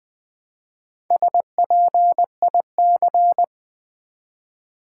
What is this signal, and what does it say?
Morse code, heard as a keyed tone, about 21 wpm, 700 Hz. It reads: SPIC
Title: Build A Word - 1 Letter At A Time - 5-Letter Words - 20wpm